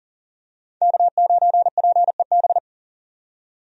Morse code, heard as a keyed tone, about 40 wpm, 700 Hz. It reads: K9PEB
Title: USA Call Signs - 5 Characters - 40wpm